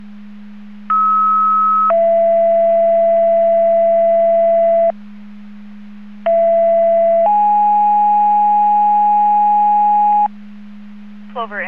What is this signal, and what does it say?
Two-tone page, heard as a one-second tone, then a three-second tone, about 1290 then 670 Hz, about 670 then 830 Hz, with speech. It is heard from 0.9 to 10.3 s.